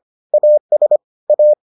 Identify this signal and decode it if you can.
Morse code, heard as a keyed tone, about 25 wpm, 600 Hz. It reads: AS A